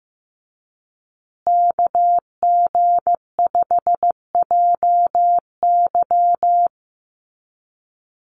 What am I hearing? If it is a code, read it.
Morse code, heard as a keyed tone, about 15 wpm, 700 Hz. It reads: KG5JY
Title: USA Call Signs Repeated 3x 15wpm